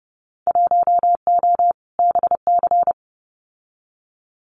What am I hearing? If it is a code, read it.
Morse code, heard as a keyed tone, about 30 wpm, 700 Hz. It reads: 1O BC
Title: Winter Field Day Exchanges Encore 30wpm